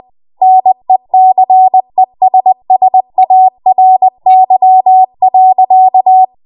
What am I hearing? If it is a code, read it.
Morse code, heard as a keyed tone, about 20 wpm, 750 Hz. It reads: NECESSARY.
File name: Morse / 20WPM